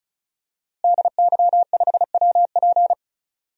Morse code, heard as a keyed tone, about 36 wpm, 700 Hz. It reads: DY5WP
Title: DXCC-100 Call Signs - 5 Characters - Repeated 3x - 35wpm